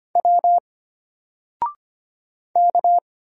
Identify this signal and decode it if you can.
Morse code, heard as a keyed tone, about 25 wpm, 700 Hz. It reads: W K
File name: Single-Letter-Number-25wpm